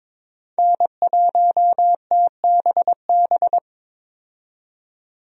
Morse code, heard as a keyed tone, about 22 wpm, 700 Hz. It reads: N1TBB
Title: USA Call Signs - 5 Characters - 22wpm